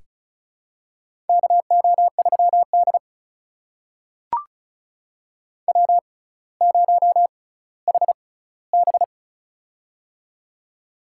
Morse code, heard as a keyed tone, about 36 wpm, spaced as for 18 wpm, 700 Hz. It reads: KO3D W0HB
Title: USA Call Signs - 4 Characters - 6x Character Spacing - 35wpm